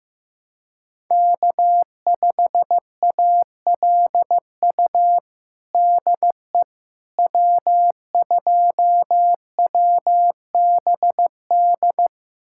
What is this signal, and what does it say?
Morse code, heard as a keyed tone, about 15 wpm, 700 Hz. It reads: K5ALU DE W2WBD
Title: USA Callsign DE Callsign - 5 Characters - 15wpm